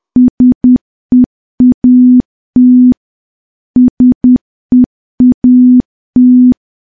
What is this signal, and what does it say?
Morse code, heard as a keyed tone, about 10 wpm, 250 Hz. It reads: SEAT SEAT
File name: morse / morse-training-10WPM